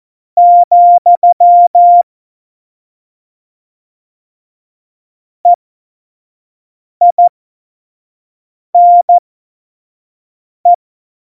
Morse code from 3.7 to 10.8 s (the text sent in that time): EINE